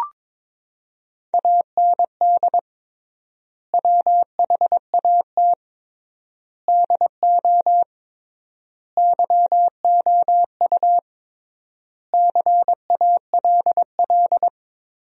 Morse code, heard as a keyed tone, about 22 wpm, 700 Hz. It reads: AND WHAT DO YOU CALL
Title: 3x Word-Spacing - Sentences from Top 100 Words - 22wpm